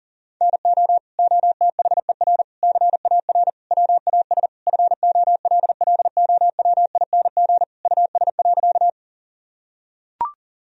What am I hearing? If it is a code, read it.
Morse code, heard as a keyed tone, about 40 wpm, 700 Hz. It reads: NO OTHER CAR WAS FOLLOWING US.
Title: Sentences from Top 1000 Words - Review Part 4 - 40wpm